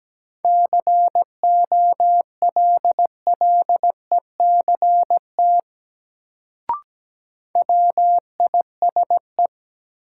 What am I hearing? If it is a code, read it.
Morse code, heard as a keyed tone, about 17 wpm, 700 Hz. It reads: COLLECT WISE
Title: Sentences from Top 1600 Words - New Word Drill - 17wpm